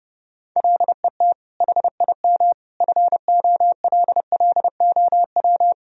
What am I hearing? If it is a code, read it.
Morse code, heard as a keyed tone, about 30 wpm, 700 Hz. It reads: LET HIM FOLLOW
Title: Sentences from Top 1000 Words - Review Part 1 - 30wpm